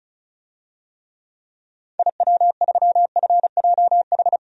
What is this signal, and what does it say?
Morse code, heard as a keyed tone, about 34 wpm, 700 Hz. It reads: IW3FJH
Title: DXCC-100 Call Signs - 6 Characters - Repeated 3x - 35wpm